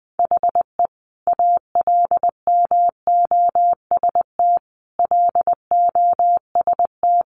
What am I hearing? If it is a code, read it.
Morse code, heard as a keyed tone, about 20 wpm, 700 Hz. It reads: HE ALMOST LOST
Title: Sentences from Top 1400 Words 20wpm